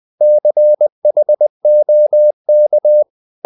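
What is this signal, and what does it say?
Morse code, heard as a keyed tone, about 20 wpm, 600 Hz. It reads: CHOK
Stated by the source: Morse code practice of short Danish words at 20 wpm, from CW med Gnister.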